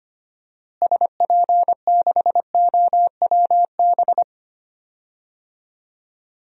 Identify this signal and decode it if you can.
Morse code, heard as a keyed tone, about 25 wpm, 700 Hz. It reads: SP6OWB
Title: DXCC-100 Call Signs - 6 Characters - Repeated 2x - 25wpm